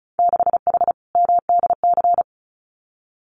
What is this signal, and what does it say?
Morse code, heard as a keyed tone, about 34 wpm, 700 Hz. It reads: 6H MDC
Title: Winter Field Day Exchanges Encore 35wpm